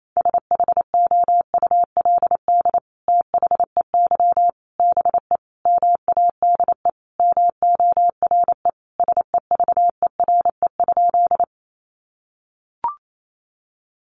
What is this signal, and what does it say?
Morse code, heard as a keyed tone, about 28 wpm, 700 Hz. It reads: SHOULD THEY BE MADE MORE SEVERE?
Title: Sentences from Top 1400 Words 28wpm